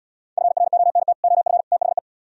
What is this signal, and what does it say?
Morse code, heard as a keyed tone, about 73 wpm, 700 Hz. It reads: FRONT OF THE